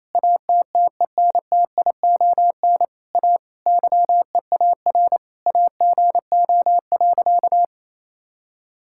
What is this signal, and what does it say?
Morse code, heard as a keyed tone, about 28 wpm, 700 Hz. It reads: ATTENTION A YEAR AGO.
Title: Sentences from Top 1000 Words - Review Part 6 - 28wpm